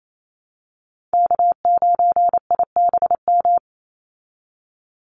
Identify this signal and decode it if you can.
Morse code, heard as a keyed tone, about 28 wpm, 700 Hz. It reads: K9IBM
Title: Call Signs Repeated 2x - 28wpm